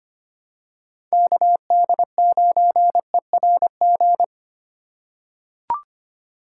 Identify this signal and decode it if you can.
Morse code, heard as a keyed tone, about 25 wpm, 690 Hz. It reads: KD9ERG
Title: USA Call Signs - 6 Characters - 3x Character Spacing - 25wpm